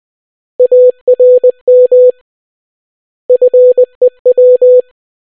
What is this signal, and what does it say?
Morse code, heard as a keyed tone, about 20 wpm, 500 Hz. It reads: ARM FEW